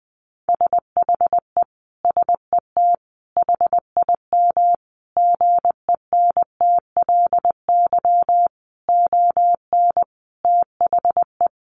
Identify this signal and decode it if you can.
Morse code, heard as a keyed tone, about 20 wpm, 700 Hz. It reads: SHE SET HIM GENTLY ON THE
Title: Sentences from Top 1500 Words 20wpm